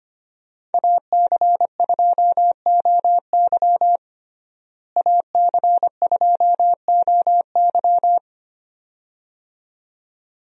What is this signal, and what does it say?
Morse code, heard as a keyed tone, about 25 wpm, 700 Hz. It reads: AC2OY AC2OY
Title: USA Call Signs - 5 Characters - Repeated 2x - 25wpm